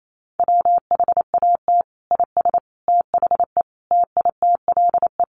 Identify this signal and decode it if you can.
Morse code, heard as a keyed tone, about 28 wpm, 700 Hz. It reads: WHAT IS THE TITLE
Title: Sentences from Top 1200 Words 28wpm